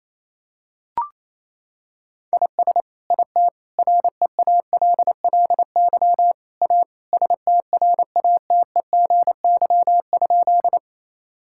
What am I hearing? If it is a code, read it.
Morse code, heard as a keyed tone, about 27 wpm, 700 Hz. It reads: IS IT REALLY A STRATEGY?
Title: Sentences from Top 1000 Words - Review Part 5 - 28wpm